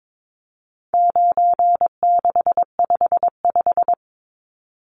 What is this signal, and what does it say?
Morse code, heard as a keyed tone, about 22 wpm, 700 Hz. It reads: 9655
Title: Numbers - 4-Digit 22wpm